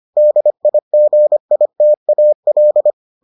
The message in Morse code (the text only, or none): DIGITAL